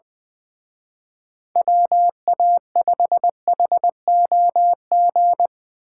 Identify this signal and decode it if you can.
Morse code, heard as a keyed tone, about 24 wpm, 700 Hz. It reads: WA5HOG
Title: Speed Racing - USA 2x3 Call Signs 20wpm